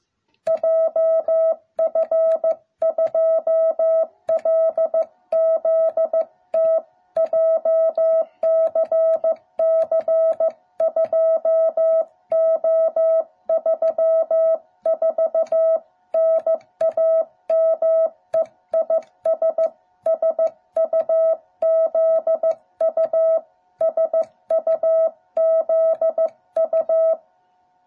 Morse code, heard as a keyed tone, about 15 wpm, 650 Hz. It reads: JF2LZT JCC2O34 NAME IS SUZU SUZU